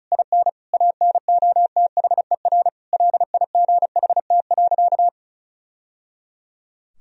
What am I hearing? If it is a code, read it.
Morse code, heard as a keyed tone, about 34 wpm, 700 Hz. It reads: IN ANOTHER LIGHT.
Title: Sentences from Top 1000 Words - Review Part 1 - 35wpm